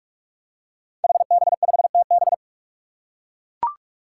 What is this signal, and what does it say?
Morse code, heard as a keyed tone, about 44 wpm, 700 Hz. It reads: HB5TB
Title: Five Challenging Letter-Number Combinations 45wpm